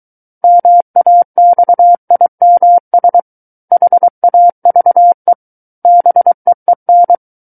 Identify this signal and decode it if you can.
Morse code, heard as a keyed tone, about 23 wpm, 700 Hz. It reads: MAXIMS HAVE BEEN
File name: Book One / 14-Chapter 14 - 23 WPM